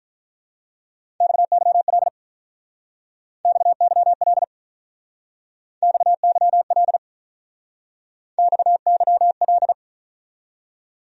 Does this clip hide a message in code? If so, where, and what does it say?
Yes: Morse code from 1.2 to 9.7 s, heard as a keyed tone, about 45 wpm, 700 Hz. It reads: XYL XYL XYL XYL